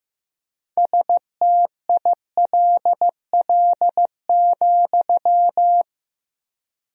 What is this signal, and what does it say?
Morse code, heard as a keyed tone, about 16 wpm, 700 Hz. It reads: STILL,
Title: 4x Word-Spacing - Sentences from Top 300 Words - 15wpm